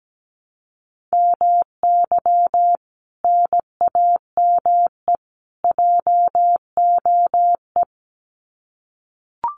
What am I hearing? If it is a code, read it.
Morse code, heard as a keyed tone, about 17 wpm, 700 Hz. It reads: MY NAME JOE
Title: Common QSO Phrases 17wpm